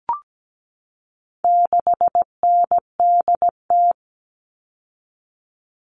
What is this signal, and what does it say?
Morse code, heard as a keyed tone, about 17 wpm, 700 Hz. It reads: 6NDT
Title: All Cut Numbers - 4-Digit 17wpm